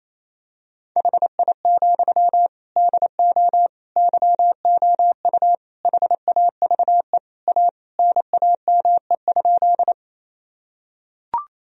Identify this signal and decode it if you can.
Morse code, heard as a keyed tone, about 28 wpm, 700 Hz. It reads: HI, DO YOU HAVE A NAME?
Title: Sentences from Top 1000 Words - Review Part 3 - 28wpm